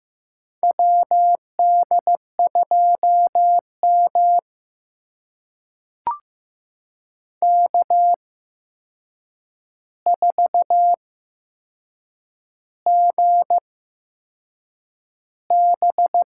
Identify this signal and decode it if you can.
Morse code, heard as a keyed tone, about 15 wpm, spaced as for 6 wpm, 700 Hz. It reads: WD2M K4GB
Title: USA Call Signs - 4 Characters - 8x Character Spacing - 15wpm